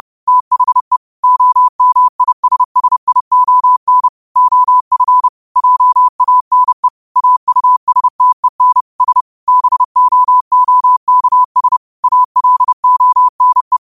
Morse code, heard as a keyed tone, about 30 wpm, 1000 Hz. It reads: THE OMISSION OF JANE AUSTEN S BOOKS ALONE
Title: 30 WPM morse code quotes for Thu, 29 Feb 2024